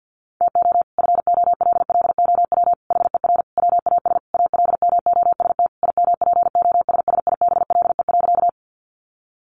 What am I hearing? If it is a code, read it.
Morse code, heard as a keyed tone, about 50 wpm, 700 Hz. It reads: TO FOLLOW HER WAS ALMOST IMPOSSIBLE.
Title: Sentences from Top 1100 Words 50wpm